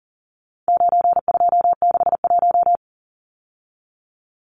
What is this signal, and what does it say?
Morse code, heard as a keyed tone, about 40 wpm, 700 Hz. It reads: 9261
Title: Numbers - 4-Digit 40wpm